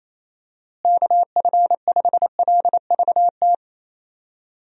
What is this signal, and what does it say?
Morse code, heard as a keyed tone, about 28 wpm, 700 Hz. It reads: KF5LVT
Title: USA Call Signs - 6 Characters - Repeated 3x - 28wpm